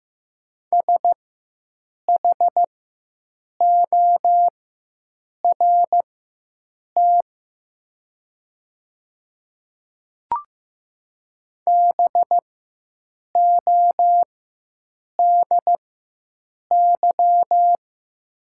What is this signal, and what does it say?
Morse code, heard as a keyed tone, about 15 wpm, spaced as for 7 wpm, 700 Hz. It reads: SHORT BODY
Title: 4x Character Spacing - Top 500 Words - 15wpm